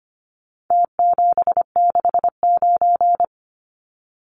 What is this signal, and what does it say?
Morse code, heard as a keyed tone, about 25 wpm, 700 Hz. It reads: T769
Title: Numbers - Contest Style - 4-Digit 25wpm